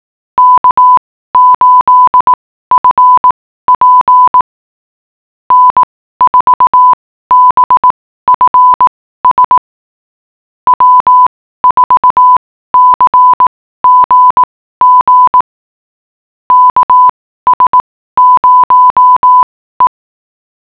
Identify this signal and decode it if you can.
Morse code, heard as a keyed tone, about 18 wpm, spaced as for 14 wpm, 1000 Hz. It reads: K8FP N4BFS W4CGG KS0E